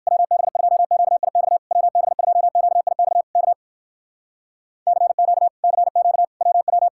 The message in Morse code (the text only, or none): WB2QIX WB2QIX K CQ CQ WW